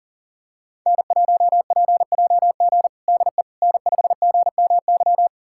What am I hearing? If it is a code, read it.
Morse code, heard as a keyed tone, about 40 wpm, 700 Hz. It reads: N1PJG DE N5GMY
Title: USA Callsign DE Callsign - 5 Characters - 40wpm